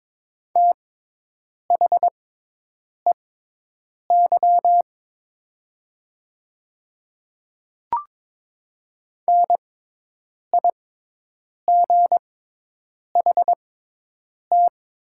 Morse code, heard as a keyed tone, about 22 wpm, spaced as for 8 wpm, 700 Hz. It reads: THEY NIGHT
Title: 6x Character Spacing - Top 300 Words - 22wpm